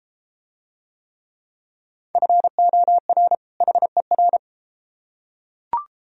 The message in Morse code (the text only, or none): FOR HER